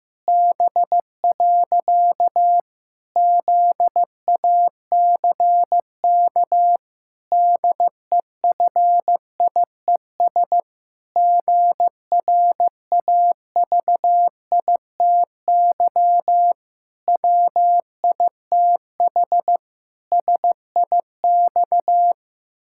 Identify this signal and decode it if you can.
Morse code, heard as a keyed tone, about 14 wpm, 700 Hz. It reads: B. ZACK DEFIES GRAVITY WITH SIX
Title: Pangram Challenge v2 - 15wpm